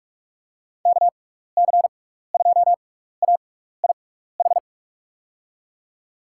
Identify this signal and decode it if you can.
Morse code, heard as a keyed tone, about 44 wpm, spaced as for 15 wpm, 700 Hz. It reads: KC2AIH
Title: USA Call Signs - 6 Characters - 6x Character Spacing - 45wpm